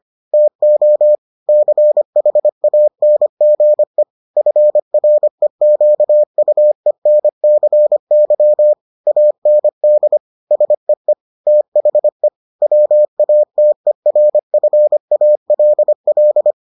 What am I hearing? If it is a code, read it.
Morse code, heard as a keyed tone, about 25 wpm, 600 Hz. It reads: TO CHANGE FREQUENCY AND SEE THE WATERFALL